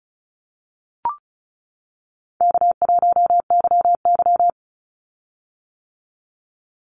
Morse code, heard as a keyed tone, about 36 wpm, 700 Hz. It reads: K1YY
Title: Call Signs - 35wpm